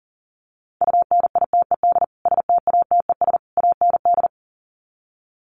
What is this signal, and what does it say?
Morse code, heard as a keyed tone, about 40 wpm, 700 Hz. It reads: UNITED STATES AND
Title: Sets of 3 Words 40wpm